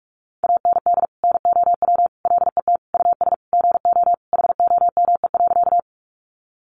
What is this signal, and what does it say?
Morse code, heard as a keyed tone, about 44 wpm, 700 Hz. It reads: AND NOW LET US GO HOME.
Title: Sentences from Top 300 Words 45wpm